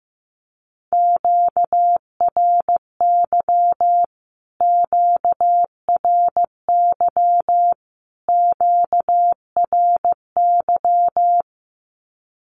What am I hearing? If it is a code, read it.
Morse code, heard as a keyed tone, about 15 wpm, 700 Hz. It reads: QRY QRY QRY
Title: Q-Codes Repeated 3x 15wpm